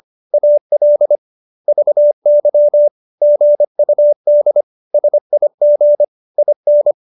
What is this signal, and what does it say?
Morse code, heard as a keyed tone, about 25 wpm, 590 Hz. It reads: AL VY GUD SIG IN